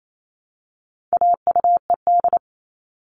Morse code, heard as a keyed tone, about 28 wpm, 700 Hz. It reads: AUED
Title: All Cut Numbers - 4-Digit 28wpm